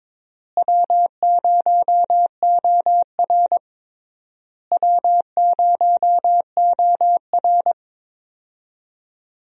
Morse code, heard as a keyed tone, about 22 wpm, 700 Hz. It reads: W0OR W0OR
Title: USA Call Signs - 4 Characters - Repeated 3x - 22wpm